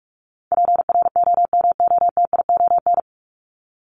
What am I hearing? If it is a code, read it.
Morse code, heard as a keyed tone, about 44 wpm, 700 Hz. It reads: PROMOTION
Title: Top 1501-1750 Words 45wpm